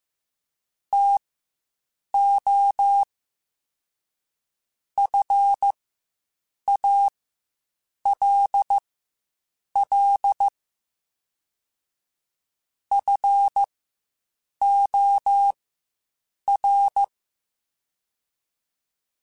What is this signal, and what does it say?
Morse code, heard as a keyed tone, about 15 wpm, spaced as for 7 wpm, 800 Hz. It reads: TO FALL FOR
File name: Quotes Spaced / Quotes Spaced-mp3s / StandFor sp @10